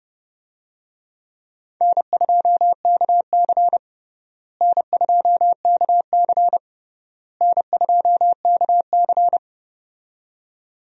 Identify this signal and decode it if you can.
Morse code, heard as a keyed tone, about 30 wpm, 700 Hz. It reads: N2KC N2KC N2KC
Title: USA Call Signs - 4 Characters - Repeated 3x - 30wpm